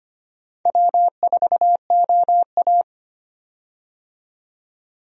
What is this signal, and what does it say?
Morse code, heard as a keyed tone, about 25 wpm, 700 Hz. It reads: W4OA
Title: USA Call Signs - 4 Characters - 25wpm